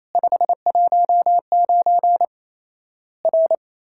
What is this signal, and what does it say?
Morse code, heard as a keyed tone, about 28 wpm, 700 Hz. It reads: 519 R
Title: Worked All Europe DX Contest - QTC Exchange - 28wpm